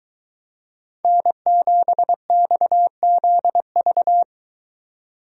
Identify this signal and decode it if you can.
Morse code, heard as a keyed tone, about 23 wpm, 700 Hz. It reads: N7XZV